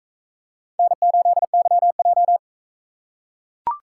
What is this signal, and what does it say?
Morse code, heard as a keyed tone, about 41 wpm, 700 Hz. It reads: N8YJ